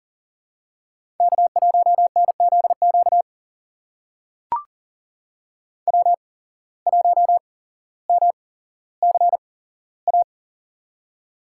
Morse code, heard as a keyed tone, about 40 wpm, spaced as for 17 wpm, 700 Hz. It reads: K1NZQ W1MCA
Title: USA Call Signs - 5 Characters - 8x Character Spacing - 40wpm